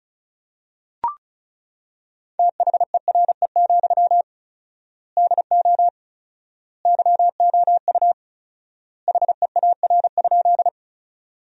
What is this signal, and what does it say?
Morse code, heard as a keyed tone, about 36 wpm, 700 Hz. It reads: THERE, DO YOU HEAR?
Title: 4x Word-Spacing - Sentences from Top 300 Words - 35wpm